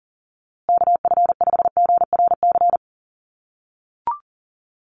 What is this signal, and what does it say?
Morse code, heard as a keyed tone, about 40 wpm, 700 Hz. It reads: KF5GRC
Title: USA Call Signs Repeated 2x 40wpm